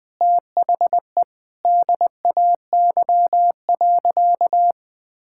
Morse code, heard as a keyed tone, about 21 wpm, 700 Hz. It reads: THE DAY.
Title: Sentences with 5-Word Phrases - Spoken Beginning - 20wpm